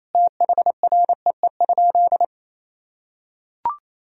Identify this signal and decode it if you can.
Morse code, heard as a keyed tone, about 27 wpm, 700 Hz. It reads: THREE?
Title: Sentences from Top 1000 Words - Review Part 2 - 28wpm